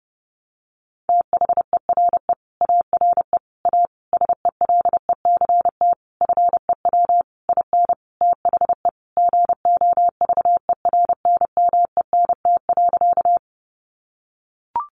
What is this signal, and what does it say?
Morse code, heard as a keyed tone, about 30 wpm, 700 Hz. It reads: THERE ARE A SELECT FEW IN THE GOVERNMENT.
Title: Sentences from Top 1300 Words 30wpm